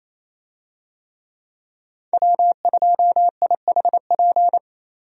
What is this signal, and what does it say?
Morse code, heard as a keyed tone, about 27 wpm, 700 Hz. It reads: W2IHP
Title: USA Call Signs - 5 Characters - 28wpm